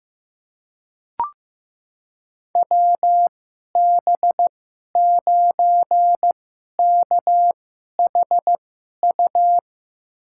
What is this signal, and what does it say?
Morse code, heard as a keyed tone, about 15 wpm, spaced as for 11 wpm, 700 Hz. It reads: WB9KHU
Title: USA Call Signs - 6 Characters - 2x Character Spacing - 15wpm